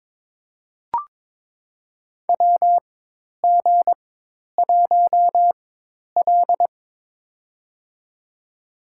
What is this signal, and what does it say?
Morse code, heard as a keyed tone, about 22 wpm, spaced as for 10 wpm, 700 Hz. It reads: WG1L
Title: USA Call Signs - 4 Characters - 4x Character Spacing - 22wpm